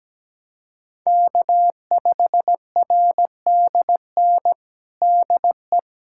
Morse code, heard as a keyed tone, about 17 wpm, 700 Hz. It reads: K5RDN DE
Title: USA Callsign DE Callsign 17wpm